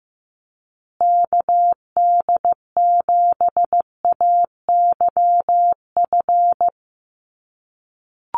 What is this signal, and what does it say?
Morse code, heard as a keyed tone, about 15 wpm, 700 Hz. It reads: KD7AYF
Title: USA Call Signs Repeated 3x 15wpm